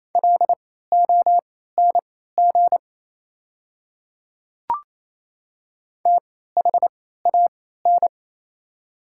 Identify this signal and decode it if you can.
Morse code, heard as a keyed tone, about 28 wpm, spaced as for 16 wpm, 700 Hz. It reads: LONG THAN
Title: 3x Character Spacing - Top 100 Words - 28wpm